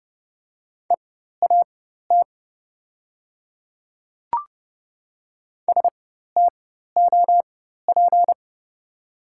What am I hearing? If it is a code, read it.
Morse code, heard as a keyed tone, about 30 wpm, spaced as for 14 wpm, 690 Hz. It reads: EAT STOP
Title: 4x Character Spacing - Top 500 Words - 30wpm